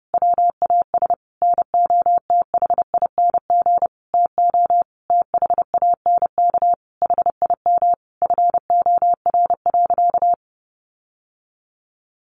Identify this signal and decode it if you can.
Morse code, heard as a keyed tone, about 30 wpm, 700 Hz. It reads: WAS NOTHING TO THANK HIM FOR.
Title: Sentences from Top 1000 Words 30wpm